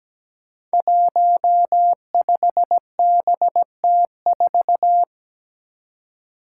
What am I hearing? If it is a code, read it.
Morse code, heard as a keyed tone, about 17 wpm, 700 Hz. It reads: 15BT4
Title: Five Challenging Letter-Number Combinations 17wpm